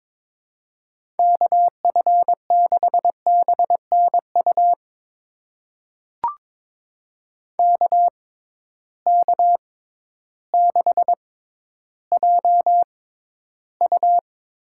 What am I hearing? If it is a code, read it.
Morse code, heard as a keyed tone, about 22 wpm, spaced as for 12 wpm, 700 Hz. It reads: KF6BNU KK6JU
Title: USA Call Signs - 6 Characters - 6x Character Spacing - 22wpm